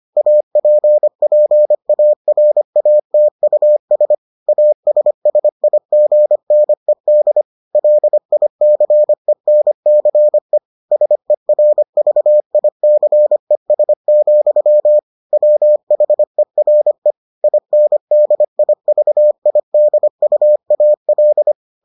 Morse code, heard as a keyed tone, about 25 wpm, 600 Hz. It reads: APPARATUS ASSIGNED LICENCE SERVICES, WHERE INDIVIDUAL